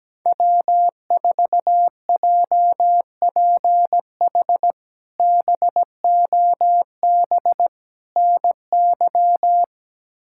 Morse code, heard as a keyed tone, about 17 wpm, 700 Hz. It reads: W4JPH BOB NY
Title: K1USN SST QSO Practice - 17wpm